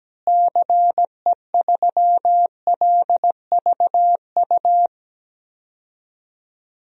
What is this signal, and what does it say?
Morse code, heard as a keyed tone, about 17 wpm, 700 Hz. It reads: CE3LVU
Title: DXCC-100 Call Signs - 6 Characters - Repeated 2x - 17wpm